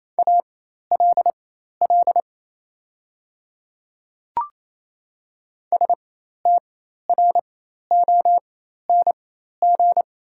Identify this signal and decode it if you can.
Morse code, heard as a keyed tone, about 28 wpm, spaced as for 13 wpm, 700 Hz. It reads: ALL STRONG
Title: 4x Character Spacing - Top 500 Words - 28wpm